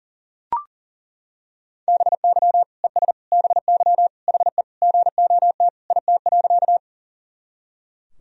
Morse code, heard as a keyed tone, about 37 wpm, 700 Hz. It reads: BY ES BY HE GOT IT.
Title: Sentences from Top 300 Words and Abbreviations 40wpm